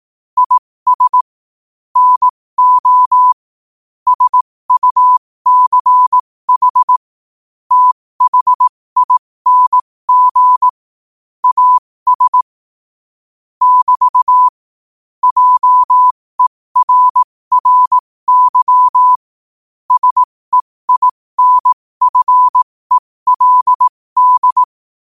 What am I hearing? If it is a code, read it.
Morse code, heard as a keyed tone, about 18 wpm, spaced as for 15 wpm, 1000 Hz. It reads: IS NO SUCH THING AS <BT> JERRY SEINFELD
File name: qotdfiles / 15wpm